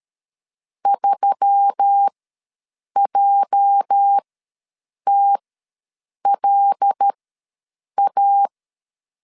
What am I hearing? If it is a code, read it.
Morse code, heard as a keyed tone, about 13 wpm, spaced as for 7 wpm, 800 Hz. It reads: 3JTLA
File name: LEZIONI / 9 2867 ws 12wpm 3431